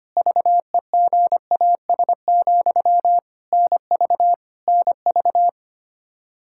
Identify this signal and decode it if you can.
Morse code, heard as a keyed tone, about 25 wpm, 700 Hz. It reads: VEGAS, NV NV
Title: K1USN SST QSO Practice 25wpm